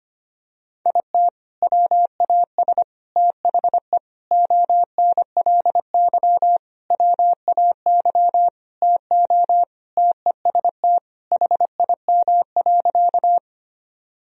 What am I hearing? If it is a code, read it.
Morse code, heard as a keyed tone, about 25 wpm, 700 Hz. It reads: IT WAS THE ONLY WAY TO TEST HIM.
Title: Sentences from Top 1000 Words - Review Part 3 - 25wpm